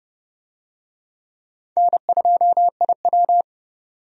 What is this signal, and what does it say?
Morse code, heard as a keyed tone, about 32 wpm, 700 Hz. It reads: N2IW
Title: USA Call Signs - 4 Characters - 30wpm